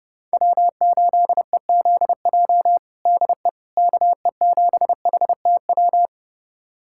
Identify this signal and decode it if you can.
Morse code, heard as a keyed tone, about 30 wpm, 700 Hz. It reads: W8EZJ DE KE7HTW
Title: USA Callsign DE Callsign 30wpm